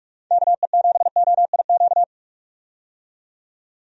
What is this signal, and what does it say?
Morse code, heard as a keyed tone, about 46 wpm, 700 Hz. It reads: KE7OIQ
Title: USA Call Signs - 6 Characters - 45wpm